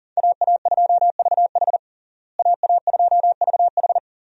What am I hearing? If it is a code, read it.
Morse code, heard as a keyed tone, about 37 wpm, 700 Hz. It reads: AA2VH AA2VH
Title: USA Call Signs - 5 Characters - Repeated 3x - 40wpm